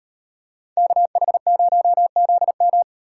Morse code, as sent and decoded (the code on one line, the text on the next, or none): -.- .... ----- --.. --
KH0ZM